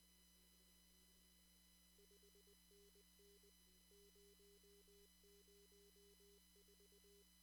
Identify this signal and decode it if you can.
Morse code, heard as a keyed tone, about 21 wpm, 400 Hz. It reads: 5NN 004
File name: CW / kt5j wpx cw-op-k5pi